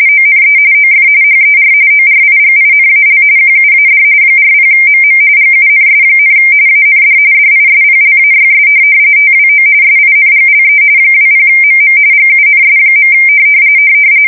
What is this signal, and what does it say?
RTTY, about 45.45 baud. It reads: ONE OF HER ORIGINAL QUAD 40MM GUNS, AND DISPLAYS TWO SINGLE 20MM GUNS.
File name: fsk170 / USS-New-Jersey